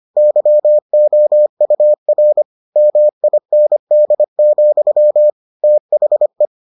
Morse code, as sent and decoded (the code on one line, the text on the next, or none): -.-- --- ..- .-. / -- .. -. -.. --..-- / - .... .
YOUR MIND, THE